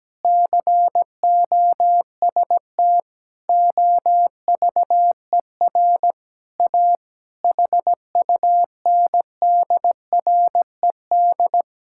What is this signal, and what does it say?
Morse code, heard as a keyed tone, about 17 wpm, 700 Hz. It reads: COST OVER A HUNDRED
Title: Sentences from Top 1000 Words - Review Part 5 - 17wpm